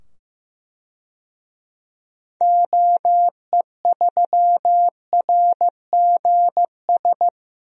Morse code, heard as a keyed tone, about 15 wpm, 690 Hz. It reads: OE3RGS
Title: DXCC-100 Call Signs - 6 Characters - 15wpm